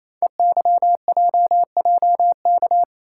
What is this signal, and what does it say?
Morse code, heard as a keyed tone, about 27 wpm, 700 Hz. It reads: EYJJK
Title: Code Groups - 5 Letters - 28wpm